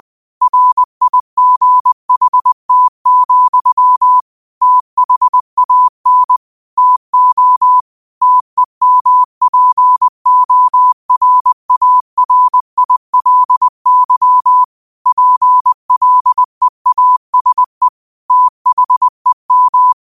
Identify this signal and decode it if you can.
Morse code, heard as a keyed tone, about 20 wpm, 1000 Hz. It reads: RIGHT, THAN TO TEMPORARILY PLEASE THEM
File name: qotdfiles / 20wpm